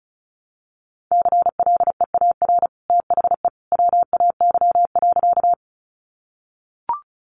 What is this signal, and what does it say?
Morse code, heard as a keyed tone, about 34 wpm, 700 Hz. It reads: CLEAR THE WAY.
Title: Sentences from Top 500 Words 35wpm